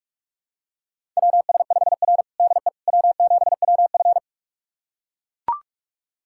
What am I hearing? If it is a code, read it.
Morse code, heard as a keyed tone, about 44 wpm, 690 Hz. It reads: WS5R DE W7WF
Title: USA Callsign DE Callsign - 4 Characters - 45wpm